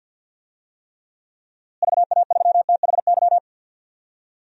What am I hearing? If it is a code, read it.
Morse code, heard as a keyed tone, about 52 wpm, 700 Hz. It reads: VA3THY
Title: DXCC-100 Call Signs - 6 Characters - Repeated 3x - 50wpm